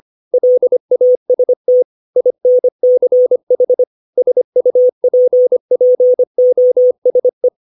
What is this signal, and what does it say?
Morse code, heard as a keyed tone, about 25 wpm, 500 Hz. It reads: LAST INCH SUPPOSE
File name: 632 words / 632 Words 25WPM